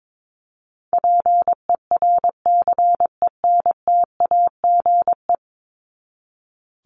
Morse code, heard as a keyed tone, about 22 wpm, 700 Hz. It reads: PERCENTAGE
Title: Top 1001-1250 Words 22wpm